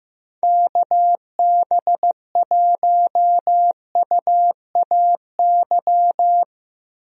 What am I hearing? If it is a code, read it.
Morse code, heard as a keyed tone, about 15 wpm, 700 Hz. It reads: KB1UAY
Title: USA Call Signs - 6 Characters - Repeated 2x - 15wpm